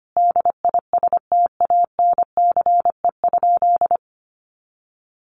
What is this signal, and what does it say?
Morse code, heard as a keyed tone, about 25 wpm, 700 Hz. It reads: DISTANCE?
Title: Sentences from Top 1200 Words 25wpm